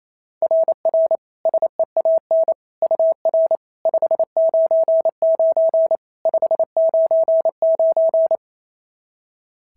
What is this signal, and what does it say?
Morse code, heard as a keyed tone, about 25 wpm, 650 Hz. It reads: RR SEAN UR 599 599
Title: SOTA QSO Practice 28wpm